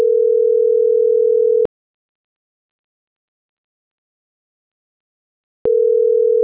Ringback tone (North America), 440 + 480 Hz.